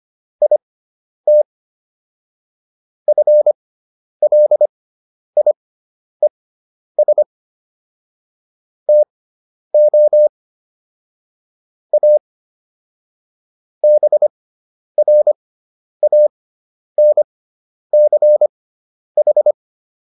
Morse code, heard as a keyed tone, about 25 wpm, spaced as for 10 wpm, 600 Hz. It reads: IT FLIES TO A BRANCH